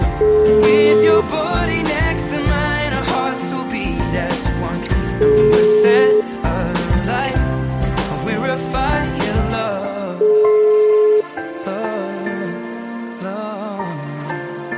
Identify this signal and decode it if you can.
Ringback tone (Europe, 425 Hz), with noise.